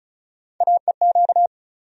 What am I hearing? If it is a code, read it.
Morse code, heard as a keyed tone, about 34 wpm, 700 Hz. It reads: AEQ
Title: USA Call Sign Postfixes - 3 Letters 35wpm